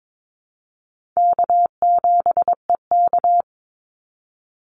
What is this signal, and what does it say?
Morse code, heard as a keyed tone, about 22 wpm, 700 Hz. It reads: K7EK
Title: Easy Call Signs Repeated 3x - 22wpm